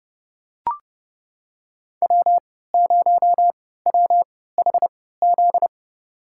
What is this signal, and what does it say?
Morse code, heard as a keyed tone, about 29 wpm, spaced as for 17 wpm, 700 Hz. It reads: W0WHZ
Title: USA Call Signs - 3x Character Spacing - 30wpm